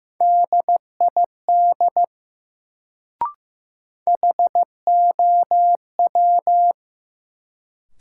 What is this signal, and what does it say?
Morse code, heard as a keyed tone, about 15 wpm, 700 Hz. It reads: DID HOW
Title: Sentences from Top 100 Words - New Word Drill - 15wpm